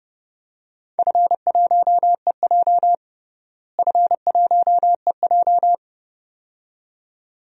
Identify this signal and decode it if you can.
Morse code, heard as a keyed tone, about 30 wpm, 700 Hz. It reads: F1EJ F1EJ
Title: DXCC-100 Call Signs - 4 Characters - Repeated 2x - 30wpm